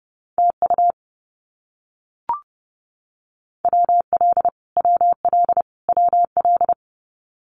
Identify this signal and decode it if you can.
Morse code, heard as a keyed tone, about 28 wpm, 700 Hz. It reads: TU WL WL WL
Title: Abbreviations Repeated 3x - 30wpm